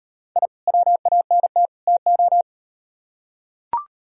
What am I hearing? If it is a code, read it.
Morse code, heard as a keyed tone, about 38 wpm, 700 Hz. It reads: I WANT TO